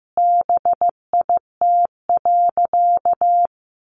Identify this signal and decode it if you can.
Morse code, heard as a keyed tone, about 15 wpm, 700 Hz. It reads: BIT.
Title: Sentences from Top 900 Words 15wpm